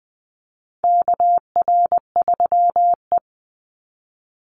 Morse code, heard as a keyed tone, about 20 wpm, 700 Hz. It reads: KR3E
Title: Call Signs - 20wpm